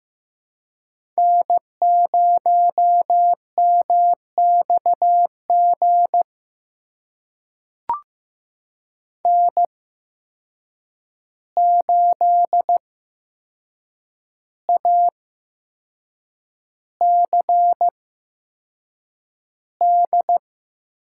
Morse code, heard as a keyed tone, about 15 wpm, spaced as for 6 wpm, 700 Hz. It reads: N0MXG N8ACD
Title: USA Call Signs - 5 Characters - 8x Character Spacing - 15wpm